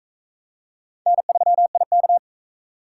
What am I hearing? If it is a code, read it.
Morse code, heard as a keyed tone, about 40 wpm, 700 Hz. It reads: N3IK